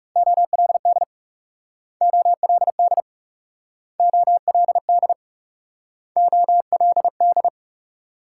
Morse code, heard as a keyed tone, about 41 wpm, 700 Hz. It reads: OLD OLD OLD OLD